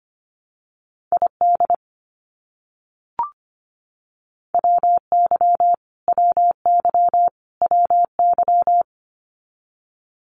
Morse code, heard as a keyed tone, about 25 wpm, 700 Hz. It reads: ID WY WY WY
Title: US States Repeated 3x 25wpm